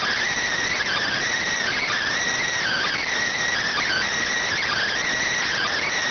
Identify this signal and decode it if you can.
Emergency Alert System (SAME) tones (header bursts), heard as harsh buzzing bursts. It is heard from 0.6 to 5.0 s.